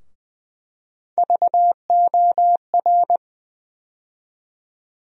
Morse code, heard as a keyed tone, about 20 wpm, 700 Hz. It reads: VOR
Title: USA Call Sign Postfixes - 3 Letters 20wpm